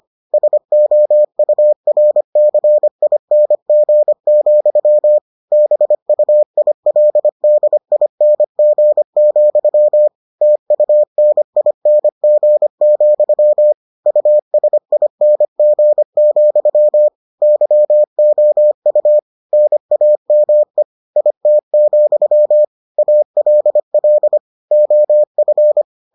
Morse code, heard as a keyed tone, about 25 wpm, 600 Hz. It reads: SOURCING, BUILDING, TUNING, USING, YOU NAME IT, ALL OF